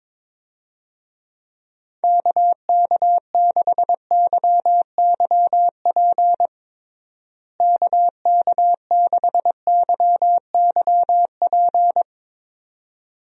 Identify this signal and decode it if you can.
Morse code, heard as a keyed tone, about 22 wpm, 700 Hz. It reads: KK6YYP KK6YYP
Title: USA Call Signs - 6 Characters - Repeated 2x - 22wpm